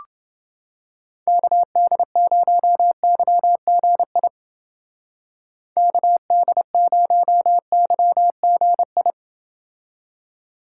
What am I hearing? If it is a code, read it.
Morse code, heard as a keyed tone, about 29 wpm, 700 Hz. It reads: KD0YGI KD0YGI